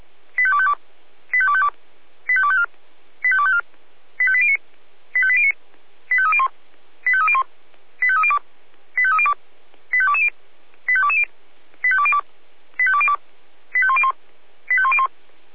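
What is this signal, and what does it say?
Five-tone selective call (ZVEI), heard as quick runs of five tones, tones of about 70 ms. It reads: 85252 85252 85254 85254 85809 85809 85301 85301 85302 85302 85220 85220 85202 85202 85101 85101
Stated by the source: ZVEI five-tone call received by SDR.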